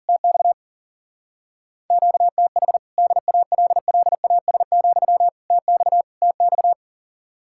Morse code, heard as a keyed tone, about 50 wpm, 700 Hz. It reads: TX QTH DALLAS, TX TX
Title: Speed-Racing Common QSO Phrases Encore - 40wpm